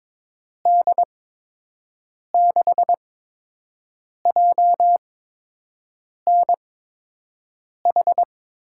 Morse code, heard as a keyed tone, about 22 wpm, spaced as for 6 wpm, 700 Hz. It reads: D6JNH
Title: USA Call Signs - 8x Character Spacing - 22wpm